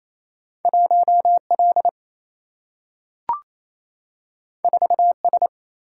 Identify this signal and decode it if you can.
Morse code, heard as a keyed tone, about 28 wpm, 700 Hz. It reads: 1 L 4 S
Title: Single-Letter Single-Number 28wpm